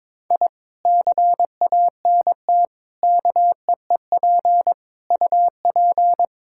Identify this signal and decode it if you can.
Morse code, heard as a keyed tone, about 22 wpm, 700 Hz. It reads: I CANT KEEP UP
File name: practice / 22wpm / Sentences with 2-Word Phrases - 22wpm